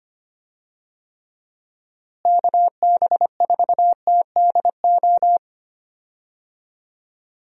Morse code, heard as a keyed tone, about 25 wpm, 700 Hz. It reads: KB4TDO
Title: USA Call Signs - 6 Characters - Repeated 3x - 25wpm